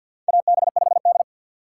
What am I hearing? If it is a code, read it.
Morse code, heard as a keyed tone, about 53 wpm, 700 Hz. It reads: AB5D
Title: USA Call Signs - 2x Character Spacing - 50wpm